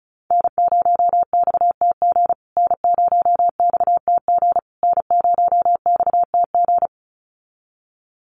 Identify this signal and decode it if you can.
Morse code, heard as a keyed tone, about 34 wpm, 700 Hz. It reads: N0XTG N0XTG N0XTG
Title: USA Call Signs Repeated 3x 35wpm